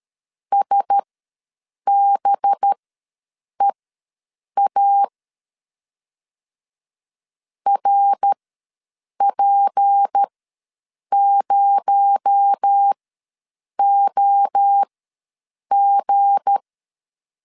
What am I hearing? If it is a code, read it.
Morse code, heard as a keyed tone, about 13 wpm, spaced as for 7 wpm, 800 Hz. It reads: SBEA RP0OG
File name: LEZIONI / 7 GF9Z / 7 G ws 12wpm 3140